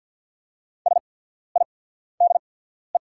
Morse code, heard as a keyed tone, about 48 wpm, spaced as for 14 wpm, 690 Hz. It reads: SIDE
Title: 8x Character Spacing - Top 500 Words - 50wpm